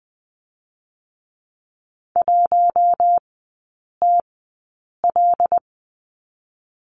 Morse code, heard as a keyed tone, about 20 wpm, spaced as for 8 wpm, 700 Hz. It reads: 1TL
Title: 127 Lesson 21e - Number 6 - Three Characters 20wpm